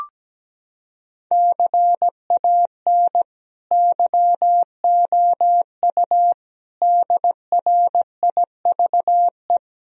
Morse code, heard as a keyed tone, about 17 wpm, 700 Hz. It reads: CAN YOU DRIVE
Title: Sentences from Top 1000 Words - Review Part 2 - 17wpm